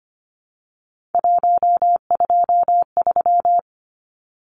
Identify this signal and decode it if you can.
Morse code, heard as a keyed tone, about 25 wpm, 700 Hz. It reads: 123
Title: Numbers - 3-Digit 25wpm